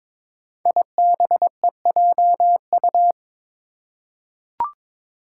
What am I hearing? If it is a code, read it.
Morse code, heard as a keyed tone, about 23 wpm, 700 Hz. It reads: IBEJU
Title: Five Challenging Letter-Number Combinations 22wpm